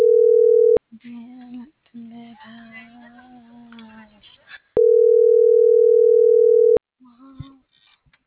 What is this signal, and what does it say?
Ringback tone (North America), 440 + 480 Hz.